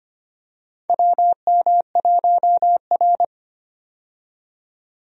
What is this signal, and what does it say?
Morse code, heard as a keyed tone, about 25 wpm, 700 Hz. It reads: WM1R
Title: USA Call Signs - 4 Characters - 25wpm